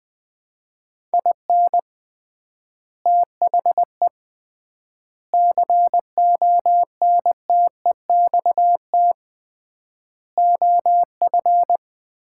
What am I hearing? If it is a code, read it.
Morse code, heard as a keyed tone, about 20 wpm, 700 Hz. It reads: IN THE CONTEXT OF
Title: 3x Word-Spacing - Sets of 5 Words - 20wpm